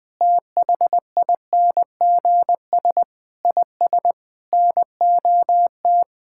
Morse code, heard as a keyed tone, about 20 wpm, 700 Hz. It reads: THINGS IS NOT
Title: Sentences with 8-Word Phrases - Spoken Beginning - 20wpm